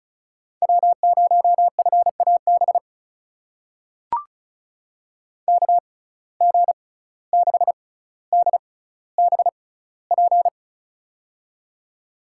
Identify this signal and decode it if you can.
Morse code, heard as a keyed tone, about 36 wpm, spaced as for 17 wpm, 700 Hz. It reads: W0FAB KG6DBP
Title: USA Call Signs - 6x Character Spacing - 35wpm